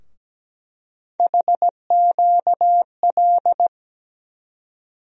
Morse code, heard as a keyed tone, about 17 wpm, 700 Hz. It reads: HQL
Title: USA Call Sign Postfixes - 3 Letters 17wpm